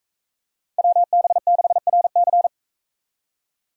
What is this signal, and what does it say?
Morse code, heard as a keyed tone, about 44 wpm, 700 Hz. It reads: WB6RC